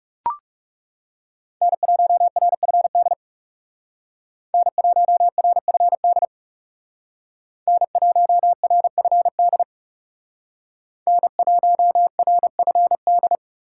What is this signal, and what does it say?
Morse code, heard as a keyed tone, about 38 wpm, 700 Hz. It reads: N1RFD N1RFD N1RFD N1RFD